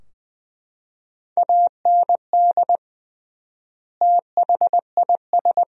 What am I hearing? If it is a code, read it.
Morse code, heard as a keyed tone, about 21 wpm, 700 Hz. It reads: AND THIS
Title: 3x Word-Spacing - Sentences from Top 300 Words - 20wpm